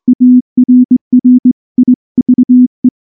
Morse code, heard as a keyed tone, about 19 wpm, 250 Hz. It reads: ARRIVE
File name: morse-training-20WPM